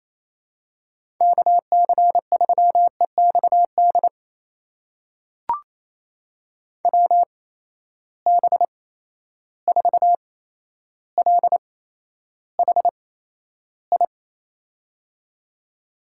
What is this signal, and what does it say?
Morse code, heard as a keyed tone, about 28 wpm, spaced as for 12 wpm, 700 Hz. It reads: KC3EXD WB4LHI
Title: USA Call Signs - 6 Characters - 8x Character Spacing - 28wpm